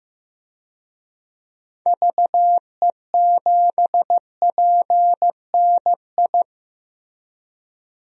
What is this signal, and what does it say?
Morse code, heard as a keyed tone, about 15 wpm, 700 Hz. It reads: VE7PNI
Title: DXCC-100 Call Signs - 6 Characters - Repeated 3x - 15wpm